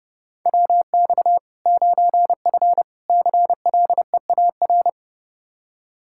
Morse code, heard as a keyed tone, about 30 wpm, 700 Hz. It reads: WX 9F CLEAR
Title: K1USN SST QSO Practice 30wpm